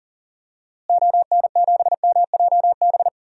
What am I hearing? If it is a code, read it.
Morse code, heard as a keyed tone, about 40 wpm, 700 Hz. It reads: ON7MJB
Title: DXCC-100 Call Signs - 6 Characters - Repeated 3x - 40wpm